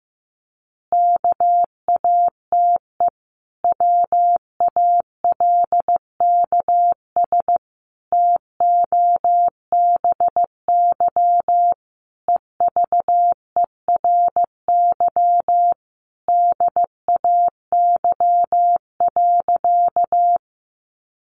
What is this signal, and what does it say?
Morse code, heard as a keyed tone, about 15 wpm, 700 Hz. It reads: KATE WALKS TOBY EVERY DAY.
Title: Easy Phrases 15wpm